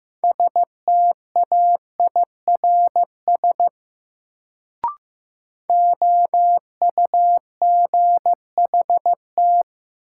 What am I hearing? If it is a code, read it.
Morse code, heard as a keyed tone, about 16 wpm, 700 Hz. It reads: STAIRS OUGHT
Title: Sets of 5 Words Encore - New Word Drill - 15wpm